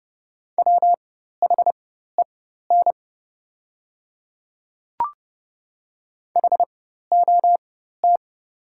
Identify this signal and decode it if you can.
Morse code, heard as a keyed tone, about 30 wpm, spaced as for 14 wpm, 700 Hz. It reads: WHEN HOT
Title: 4x Character Spacing - Top 100 Words - 30wpm